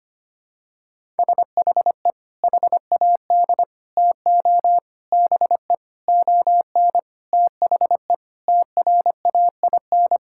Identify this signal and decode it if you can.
Morse code, heard as a keyed tone, about 25 wpm, 700 Hz. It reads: SHE HAD TO BE ON THE TRAIN
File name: practice / 25wpm / Sentences with 4-Word Phrases - 25wpm